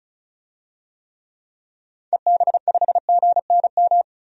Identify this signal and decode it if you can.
Morse code, heard as a keyed tone, about 34 wpm, 700 Hz. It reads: EB5GNM
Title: DXCC-100 Call Signs - 6 Characters - 35wpm